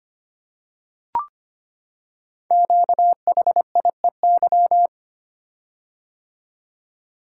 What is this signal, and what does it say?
Morse code, heard as a keyed tone, about 25 wpm, 690 Hz. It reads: QHIEY
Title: Code Groups - 5 Letters - 25wpm